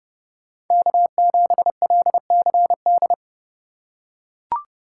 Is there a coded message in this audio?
Yes: Morse code from 0.7 to 3.1 s, heard as a keyed tone, about 30 wpm, 700 Hz. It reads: K7LCD